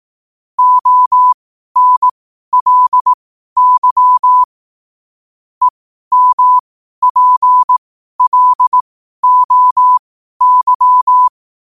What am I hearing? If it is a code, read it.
Morse code, heard as a keyed tone, about 18 wpm, spaced as for 12 wpm, 1000 Hz. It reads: ONLY EMPLOY
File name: qotdfiles / 12wpm